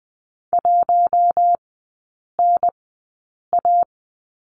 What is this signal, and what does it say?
Morse code, heard as a keyed tone, about 20 wpm, spaced as for 8 wpm, 700 Hz. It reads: 1NA
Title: 053 Lesson 10e - Letter r - Three Characters 20wpm